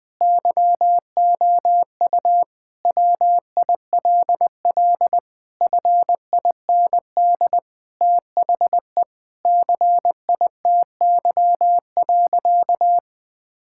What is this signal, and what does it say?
Morse code, heard as a keyed tone, about 21 wpm, 700 Hz. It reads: YOU WILL FIND THE CITY.
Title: Sentences from Top 1000 Words - Review Part 6 - 20wpm